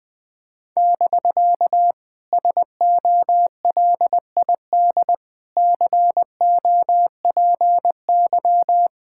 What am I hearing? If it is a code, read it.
Morse code, heard as a keyed tone, about 20 wpm, 700 Hz. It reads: <BK> SOLID COPY
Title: Speed-Racing Common QSO Phrases - 20wpm